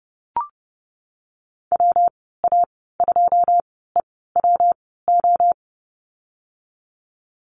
Morse code, heard as a keyed tone, about 32 wpm, spaced as for 17 wpm, 700 Hz. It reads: WA2EWO
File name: Lesson 19g - Letter w - Call Signs 30wpm